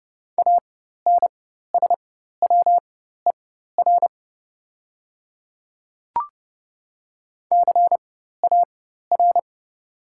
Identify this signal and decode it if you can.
Morse code, heard as a keyed tone, about 30 wpm, spaced as for 14 wpm, 700 Hz. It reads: ANSWER CAR
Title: 4x Character Spacing - Top 300 Words - 30wpm